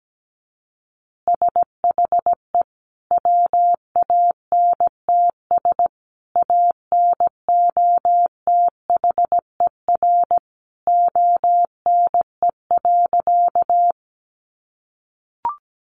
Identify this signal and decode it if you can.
Morse code, heard as a keyed tone, about 17 wpm, 700 Hz. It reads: SHE WANTS ANOTHER ONE.
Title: Easy Phrases 17wpm